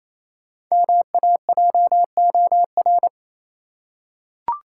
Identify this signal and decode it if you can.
Morse code, heard as a keyed tone, about 28 wpm, 700 Hz. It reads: MAJOR
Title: Five Letter Words - 28wpm